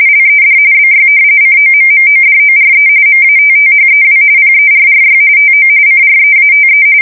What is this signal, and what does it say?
RTTY, about 45.45 baud. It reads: LED BY COMMODORE MATTHEW PERRY TO CREATE
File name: fsk170 / usnavy2